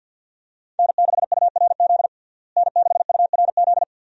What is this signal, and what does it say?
Morse code, heard as a keyed tone, about 46 wpm, 700 Hz. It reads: N6URZ N6URZ
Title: USA Call Signs - 5 Characters - Repeated 2x - 50wpm